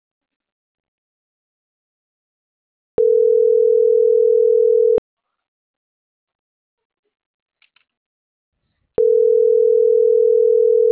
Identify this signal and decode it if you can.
Ringback tone (North America), 440 + 480 Hz.